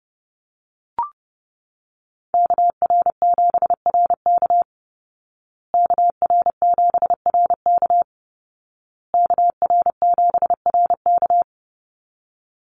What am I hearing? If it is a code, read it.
Morse code, heard as a keyed tone, about 30 wpm, 700 Hz. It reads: KR7RK KR7RK KR7RK
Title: Call Signs Repeated 2x - 30wpm